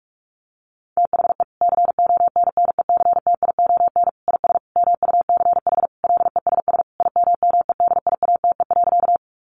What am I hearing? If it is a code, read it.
Morse code, heard as a keyed tone, about 44 wpm, 700 Hz. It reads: THE CONNECTION IS MUCH LESS IMMEDIATE.
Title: Sentences from Top 1300 Words 45wpm